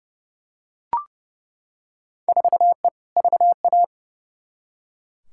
Morse code, heard as a keyed tone, about 32 wpm, 700 Hz. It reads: 4E VA
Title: Field Day Exchanges Encore - 30wpm